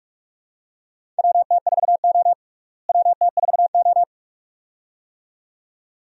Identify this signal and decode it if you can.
Morse code, heard as a keyed tone, about 42 wpm, 700 Hz. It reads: WT4O WT4O
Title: USA Call Signs - 4 Characters - Repeated 2x - 45wpm